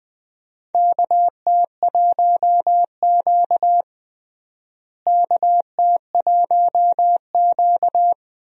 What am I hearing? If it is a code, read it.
Morse code, heard as a keyed tone, about 20 wpm, 700 Hz. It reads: KT1Q KT1Q
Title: USA Call Signs - 4 Characters - Repeated 3x - 20wpm